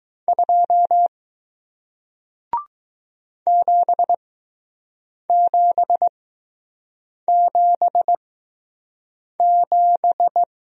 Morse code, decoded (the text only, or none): 2 7777